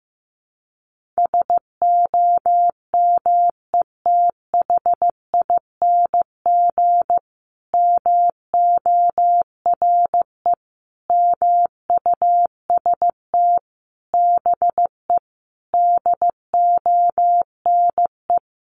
Morse code, heard as a keyed tone, about 15 wpm, 700 Hz. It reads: SOMETHING MORE MUST BE DONE
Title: Sentences from Top 1400 Words 15wpm